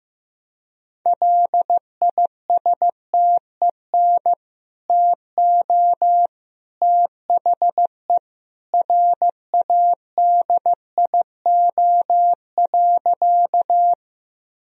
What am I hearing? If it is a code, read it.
Morse code, heard as a keyed tone, about 15 wpm, 700 Hz. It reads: LISTEN TO THE RADIO.
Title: Sentences from Top 1000 Words - Review Part 3 - 15wpm